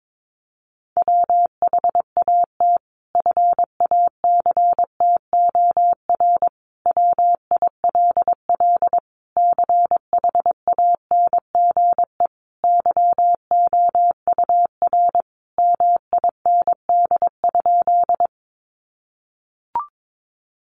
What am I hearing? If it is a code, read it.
Morse code, heard as a keyed tone, about 22 wpm, 700 Hz. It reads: WHAT FACTOR WILL CHANGE YOUR MIND?
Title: Sentences from Top 1300 Words 22wpm